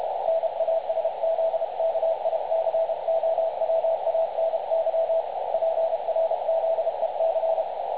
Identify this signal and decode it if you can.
Morse code, heard as a keyed tone, about 30 wpm, 650 Hz, with radio noise. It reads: NASZKICOWANY PRZEC
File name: dlazaawansowanych / nagraniazpasma / websdr recording 3565.0kHz